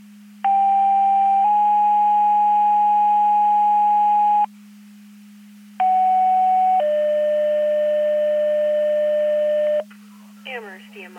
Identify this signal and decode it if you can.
Two-tone page, heard as a one-second tone, then a three-second tone, about 790 then 830 Hz, about 750 then 600 Hz, with speech. It is heard from 0.4 to 9.8 s.